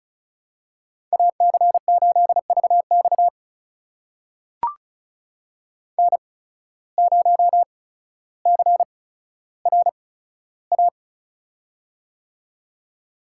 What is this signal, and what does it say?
Morse code, heard as a keyed tone, about 36 wpm, spaced as for 15 wpm, 700 Hz. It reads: AC8VX N0CRA
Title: USA Call Signs - 5 Characters - 8x Character Spacing - 35wpm